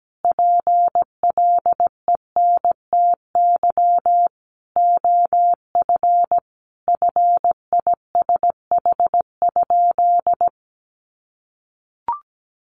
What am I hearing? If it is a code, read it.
Morse code, heard as a keyed tone, about 17 wpm, 700 Hz. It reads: PLENTY OF FISH?
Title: Sentences from Top 1500 Words 17wpm